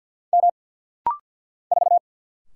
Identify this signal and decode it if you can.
Morse code, heard as a keyed tone, about 50 wpm, 690 Hz. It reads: M 4
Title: Single Letter-Number - Rapid-Fire 50wpm